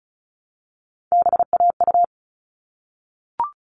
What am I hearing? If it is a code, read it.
Morse code, heard as a keyed tone, about 34 wpm, 700 Hz. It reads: BAU
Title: All Cut Numbers - 3-Digit 35wpm